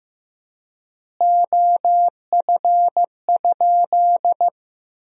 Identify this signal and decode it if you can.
Morse code, heard as a keyed tone, about 15 wpm, 700 Hz. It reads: OF?
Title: 3x Word-Spacing - Sentences from Top 200 Words - 15wpm